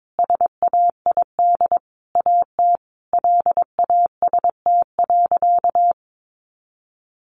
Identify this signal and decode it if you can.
Morse code, heard as a keyed tone, about 22 wpm, 700 Hz. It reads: SAID AT LAST.
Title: A Sound Of Thunder 22wpm